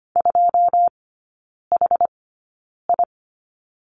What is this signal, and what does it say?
Morse code, heard as a keyed tone, about 25 wpm, spaced as for 9 wpm, 700 Hz. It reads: 2HI
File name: course / 25:10wpm / 093 Lesson 16e - Letter c - Three Characters 25wpm